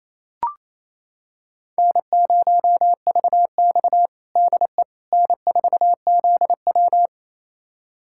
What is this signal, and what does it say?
Morse code, heard as a keyed tone, about 28 wpm, 700 Hz. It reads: N0VX DE N4ZW
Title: USA Callsign DE Callsign - 4 Characters - 28wpm